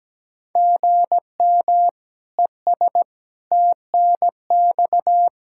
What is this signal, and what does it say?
Morse code, heard as a keyed tone, about 17 wpm, 700 Hz. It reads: GM ES TNX
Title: Practice QSO - Exchange 1 - Beginning - 17wpm